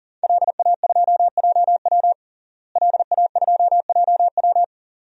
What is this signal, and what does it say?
Morse code, heard as a keyed tone, about 37 wpm, 690 Hz. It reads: LA2JW LA2JW
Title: DXCC-100 Call Signs - 5 Characters - Repeated 2x - 40wpm